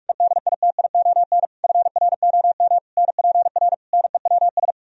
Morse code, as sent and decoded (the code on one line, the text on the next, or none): . -.. .. - .. --- -. / ..-. .-. --- -- / -. .--. .-. / -. .-- ...
EDITION FROM NPR NEWS